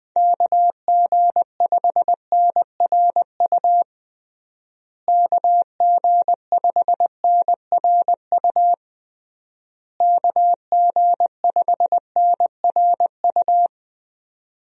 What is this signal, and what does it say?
Morse code, heard as a keyed tone, about 20 wpm, 700 Hz. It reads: KG5NRU KG5NRU KG5NRU
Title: USA Call Signs - 6 Characters - Repeated 3x - 20wpm